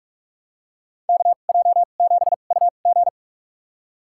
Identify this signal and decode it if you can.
Morse code, heard as a keyed tone, about 47 wpm, spaced as for 33 wpm, 700 Hz. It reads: KJ7UG
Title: USA Call Signs - 2x Character Spacing - 45wpm